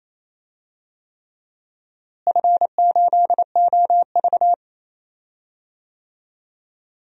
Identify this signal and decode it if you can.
Morse code, heard as a keyed tone, about 28 wpm, 700 Hz. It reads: F8OV
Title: DXCC-100 Call Signs - 4 Characters - Repeated 3x - 28wpm